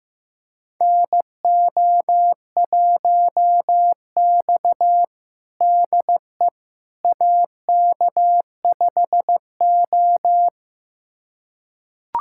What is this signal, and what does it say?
Morse code, heard as a keyed tone, about 15 wpm, 700 Hz. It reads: NO1X DE AK5O
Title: USA Callsign DE Callsign - 4 Characters - 15wpm